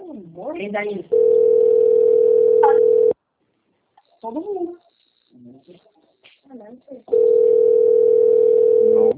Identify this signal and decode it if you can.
Ringback tone (North America), 440 + 480 Hz, with speech.
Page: Capital - SP